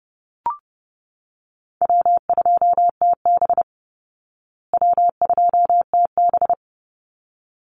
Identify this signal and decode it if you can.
Morse code, heard as a keyed tone, about 30 wpm, 700 Hz. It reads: W2TB W2TB
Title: Call Signs Repeated 3x - 30wpm